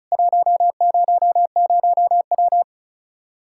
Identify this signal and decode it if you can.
Morse code, heard as a keyed tone, about 36 wpm, 700 Hz. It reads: 100W
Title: Practice QSO - Exchange 2 - Info - 35wpm